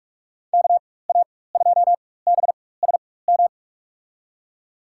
Morse code, heard as a keyed tone, about 46 wpm, spaced as for 21 wpm, 700 Hz. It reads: KA2BSM
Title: USA Call Signs - 6 Characters - 4x Character Spacing - 45wpm